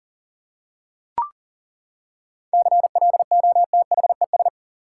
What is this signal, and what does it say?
Morse code, heard as a keyed tone, about 40 wpm, 700 Hz. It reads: CLOTHES
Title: Seven Letter Words - 40wpm